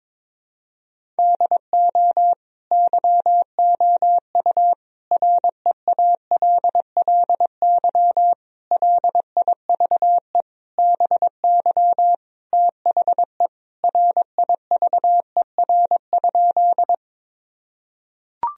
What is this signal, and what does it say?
Morse code, heard as a keyed tone, about 21 wpm, 700 Hz. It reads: DO YOU REALLY LIVE BY THE RIVER?
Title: Sentences from Top 1000 Words - Review Part 1 - 22wpm